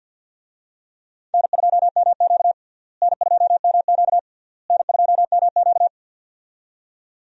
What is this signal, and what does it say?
Morse code, heard as a keyed tone, about 52 wpm, 700 Hz. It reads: N2MQ N2MQ N2MQ
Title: USA Call Signs - 4 Characters - Repeated 3x - 50wpm